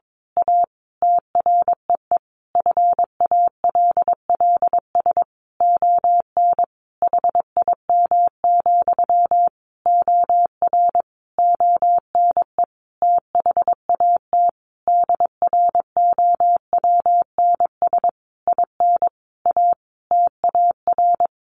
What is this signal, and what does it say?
Morse code, heard as a keyed tone, about 22 wpm, 700 Hz. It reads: A TREE FALLS ON HIM, OR ONE THAT DROWNS IN A TAR
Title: A Sound Of Thunder 22wpm